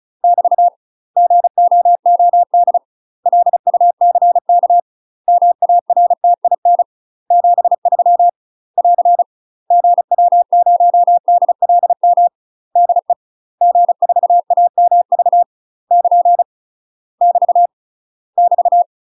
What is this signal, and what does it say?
Morse code, heard as a keyed tone, about 35 wpm, 700 Hz. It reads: <BT> GOOD LUCK MARTIN 73 <AR> GW0DLM DE G4AMV <KN> <BT> <BT>
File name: UK MASTER QSOs TRAINING / 35WPM / UK MASTER QSOs 0036